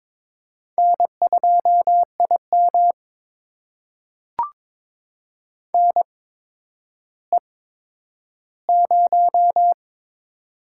Morse code, heard as a keyed tone, about 22 wpm, spaced as for 11 wpm, 700 Hz. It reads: N2IM NE0
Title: USA Call Signs - 4 Characters - 8x Character Spacing - 22wpm